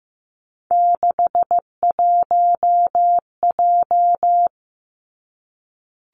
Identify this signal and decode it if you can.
Morse code, heard as a keyed tone, about 15 wpm, 700 Hz. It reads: 61J
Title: Three Challenging Letter-Number Combinations 15wpm